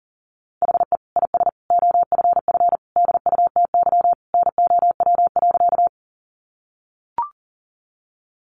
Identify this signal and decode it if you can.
Morse code, heard as a keyed tone, about 40 wpm, 700 Hz. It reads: HE IS OFF DUTY NOW.
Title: Sentences from Top 1500 Words 40wpm